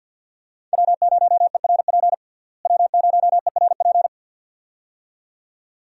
Morse code, heard as a keyed tone, about 50 wpm, 700 Hz. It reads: W0ERP W0ERP
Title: USA Call Signs - 5 Characters - Repeated 2x - 50wpm